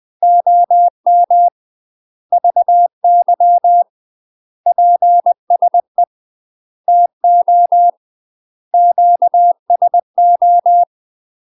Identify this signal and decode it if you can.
Morse code, heard as a keyed tone, about 20 wpm, 700 Hz. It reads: OM VY PSE TO QSO
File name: UK MASTER QSOs TRAINING / 20WPM / UK MASTER QSOs 0063